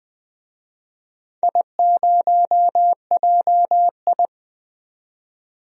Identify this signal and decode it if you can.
Morse code, heard as a keyed tone, about 21 wpm, 700 Hz. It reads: I0JI
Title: DXCC-100 Call Signs - 4 Characters - 20wpm